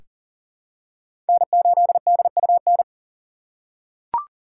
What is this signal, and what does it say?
Morse code, heard as a keyed tone, about 40 wpm, 700 Hz. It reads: N8DUN
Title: USA Call Signs - 5 Characters - 3x Character Spacing - 40wpm